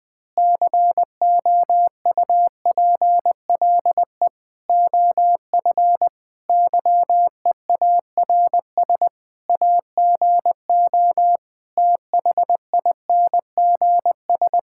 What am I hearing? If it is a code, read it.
Morse code, heard as a keyed tone, about 20 wpm, 700 Hz. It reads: COUPLE OF YEARS AGO THINGS
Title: Sentences with 5-Word Phrases - Spoken Beginning - 20wpm